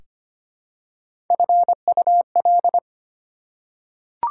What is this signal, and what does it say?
Morse code, heard as a keyed tone, about 25 wpm, 700 Hz. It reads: FUL
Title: Prefixes and Suffixes 25wpm